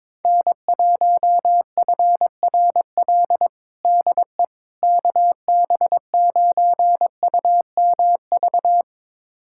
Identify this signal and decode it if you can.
Morse code, heard as a keyed tone, about 22 wpm, 700 Hz. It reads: N1FRL DE KB9UMV
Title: USA Callsign DE Callsign 22wpm